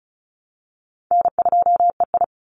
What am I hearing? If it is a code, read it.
Morse code, heard as a keyed tone, about 34 wpm, 700 Hz. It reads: N2EI
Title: Call Signs Repeated 3x - 35wpm